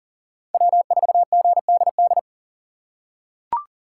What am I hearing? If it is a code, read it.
Morse code, heard as a keyed tone, about 40 wpm, 700 Hz. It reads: W4GDD
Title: USA Call Signs - 5 Characters - 4x Character Spacing - 40wpm